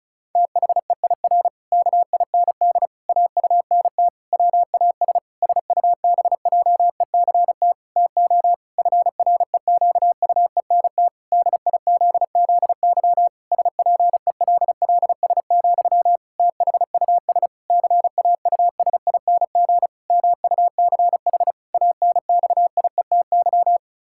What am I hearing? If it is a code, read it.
Morse code, heard as a keyed tone, about 36 wpm, 700 Hz. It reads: THEIR KIND AUNT WAS SUBJECT TO FREQUENT DIZZY SPELLS, THUS CAUSING MUCH ANXIETY
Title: Pangram Challenge v2 - 35wpm